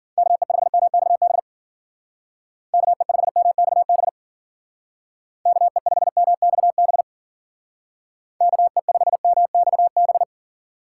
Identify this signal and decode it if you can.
Morse code, heard as a keyed tone, about 50 wpm, 700 Hz. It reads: KE5MXB KE5MXB KE5MXB KE5MXB
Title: Speed Racing - USA 2x3 Call Signs 40wpm